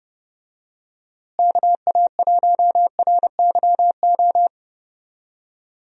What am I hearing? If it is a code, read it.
Morse code, heard as a keyed tone, about 29 wpm, 700 Hz. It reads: KA1RYO
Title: USA Call Signs - 6 Characters - Repeated 3x - 30wpm